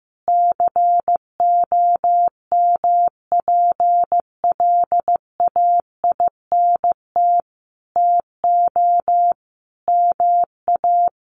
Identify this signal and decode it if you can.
Morse code, heard as a keyed tone, about 15 wpm, 700 Hz. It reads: COMPLAINT TO MA
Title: Sentences from Top 1400 Words 15wpm